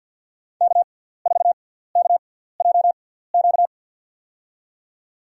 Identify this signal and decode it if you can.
Morse code, heard as a keyed tone, about 52 wpm, spaced as for 17 wpm, 700 Hz. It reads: K4KJQ
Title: USA Call Signs - 6x Character Spacing - 50wpm